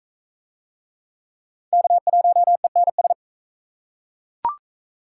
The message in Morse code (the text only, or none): K1ENS